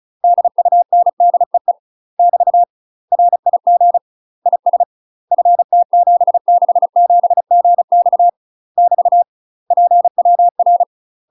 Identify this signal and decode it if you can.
Morse code, heard as a keyed tone, about 34 wpm, 700 Hz. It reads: DUNDEE <BT> RIG IS FT767GX <BT> PWR
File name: UK MASTER QSOs TRAINING / 35WPM / UK MASTER QSOs 0004